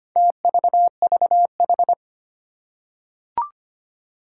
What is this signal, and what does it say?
Morse code, heard as a keyed tone, about 25 wpm, 700 Hz. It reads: TVVH
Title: Four Challenging Letter-Number Combinations 25wpm